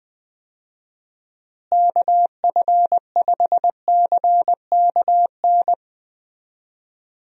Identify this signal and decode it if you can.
Morse code, heard as a keyed tone, about 20 wpm, 700 Hz. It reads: KF5CKN
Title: USA Call Signs - 6 Characters - Repeated 2x - 20wpm